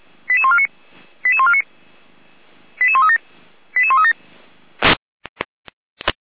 Five-tone selective call (ZVEI), heard as quick runs of five tones, tones of about 70 ms. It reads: 88149 88149 88137 88137